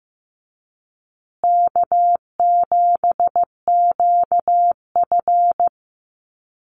Morse code, heard as a keyed tone, about 15 wpm, 700 Hz. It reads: K7QF
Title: USA Call Signs Repeated 2x 15wpm